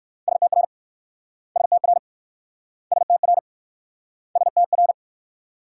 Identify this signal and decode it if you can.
Morse code, heard as a keyed tone, about 64 wpm, 690 Hz. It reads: STR STR STR STR